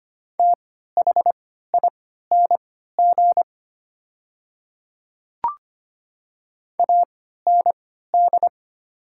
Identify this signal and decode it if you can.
Morse code, heard as a keyed tone, about 25 wpm, spaced as for 14 wpm, 700 Hz. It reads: THING AND
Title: 3x Character Spacing - Top 300 Words - 25wpm